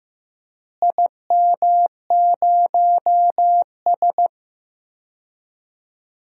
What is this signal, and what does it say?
Morse code, heard as a keyed tone, about 15 wpm, 700 Hz. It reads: IM0S
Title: DXCC-100 Call Signs - 4 Characters - Repeated 2x - 15wpm